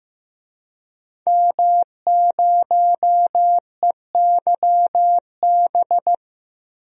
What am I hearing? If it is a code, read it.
Morse code, heard as a keyed tone, about 15 wpm, 700 Hz. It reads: M0EYB
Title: DXCC-100 Call Signs - 5 Characters - 15wpm